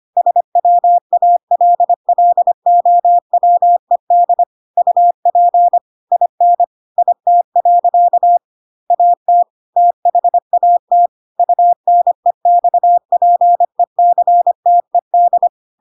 Morse code, heard as a keyed tone, about 25 wpm, 700 Hz. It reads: SWALLOWED UP IN IT. AT THAT UNEXPECTED
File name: Pinocchio / Pinocchio ch3 25 25 ik1hge